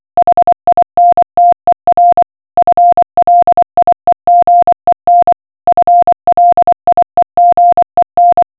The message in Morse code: HINTER FLIEGEN FLIEGEN